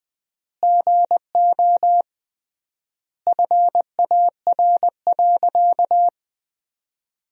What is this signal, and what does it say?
Morse code, heard as a keyed tone, about 20 wpm, 700 Hz. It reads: GO FAR.
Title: 3x Word-Spacing - Sentences from Top 400 Words - 20wpm